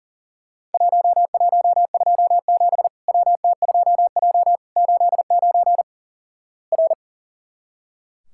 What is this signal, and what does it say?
Morse code, heard as a keyed tone, about 37 wpm, 700 Hz. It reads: 1127 WT2J 89 R
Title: Worked All Europe DX Contest - QTC Exchange - 40wpm